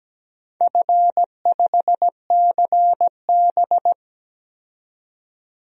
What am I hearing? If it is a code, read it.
Morse code, heard as a keyed tone, about 17 wpm, 700 Hz. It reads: F5CB
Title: DXCC-100 Call Signs - 4 Characters - 17wpm